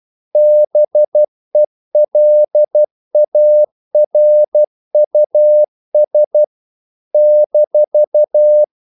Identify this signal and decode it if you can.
Morse code, heard as a keyed tone, about 12 wpm, 600 Hz. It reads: BELARUS -
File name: CW / Practice / NationalCapitals12